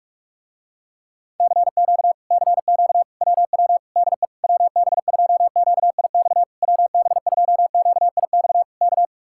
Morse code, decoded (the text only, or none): CQ CQ WW DE WB2QIX WB2QIX K